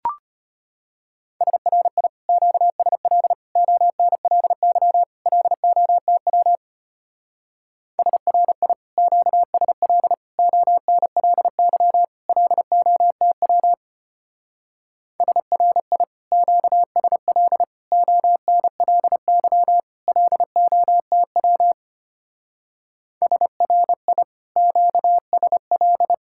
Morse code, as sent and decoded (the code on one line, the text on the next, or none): ... .-. .. / --.- ... .-.. / --- -. .-.. -.-- / .-.. --- - .-- / ... .-. .. / --.- ... .-.. / --- -. .-.. -.-- / .-.. --- - .-- / ... .-. .. / --.- ... .-.. / --- -. .-.. -.-- / .-.. --- - .-- / ... .-. .. / --.- ... .-..
SRI QSL ONLY LOTW SRI QSL ONLY LOTW SRI QSL ONLY LOTW SRI QSL